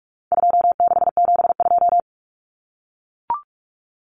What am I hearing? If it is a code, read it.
Morse code, heard as a keyed tone, about 44 wpm, 700 Hz. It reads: 2672
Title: Numbers - 4-Digit 45wpm